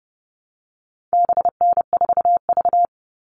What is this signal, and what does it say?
Morse code, heard as a keyed tone, about 30 wpm, 700 Hz. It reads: BN4V
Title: All Cut Numbers - 4-Digit 30wpm